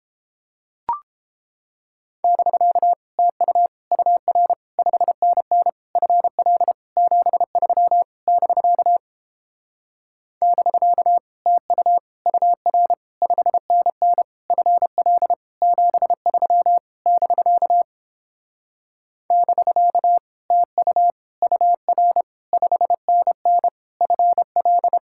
Morse code, heard as a keyed tone, about 29 wpm, 700 Hz. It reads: <BK> TU UR 5NN FL 73 <BK> <BK> TU UR 5NN FL 73 <BK> <BK> TU UR 5NN FL